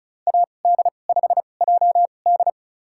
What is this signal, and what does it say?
Morse code, heard as a keyed tone, about 36 wpm, spaced as for 26 wpm, 700 Hz. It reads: AD5JD
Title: USA Call Signs - 5 Characters - 2x Character Spacing - 35wpm